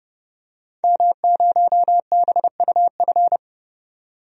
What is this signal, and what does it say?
Morse code, heard as a keyed tone, about 30 wpm, 700 Hz. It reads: M0BUF
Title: DXCC-100 Call Signs - 5 Characters - 30wpm